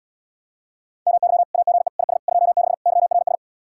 Morse code, heard as a keyed tone, about 77 wpm, 700 Hz. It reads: MY AGE IS 26 YRS